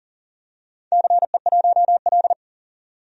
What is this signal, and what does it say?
Morse code, heard as a keyed tone, about 39 wpm, 700 Hz. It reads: CE1L
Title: DXCC-100 Call Signs - 4 Characters - 40wpm